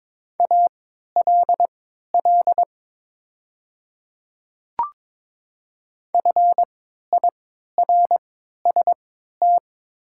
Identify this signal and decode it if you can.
Morse code, heard as a keyed tone, about 22 wpm, spaced as for 12 wpm, 700 Hz. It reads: ALL FIRST
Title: 3x Character Spacing - Top 100 Words - 22wpm